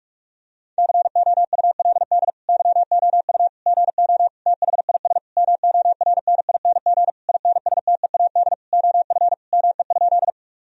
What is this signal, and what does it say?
Morse code, heard as a keyed tone, about 43 wpm, 700 Hz. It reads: COULD YOU GO THIS MORNING INSTEAD OF ME?
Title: Sentences from Top 1000 Words - Review Part 4 - 45wpm